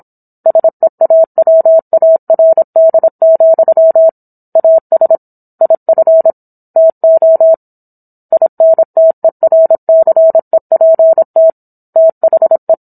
Morse code, heard as a keyed tone, about 26 wpm, 650 Hz. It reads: SEAWARD, AS IF TO INTERCEPT THE